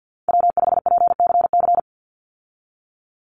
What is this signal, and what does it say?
Morse code, heard as a keyed tone, about 50 wpm, 700 Hz. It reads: W5PCC
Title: USA Call Signs 50wpm